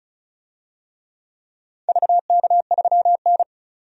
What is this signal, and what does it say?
Morse code, heard as a keyed tone, about 36 wpm, 700 Hz. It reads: VK3N
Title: DXCC-100 Call Signs - 4 Characters - 35wpm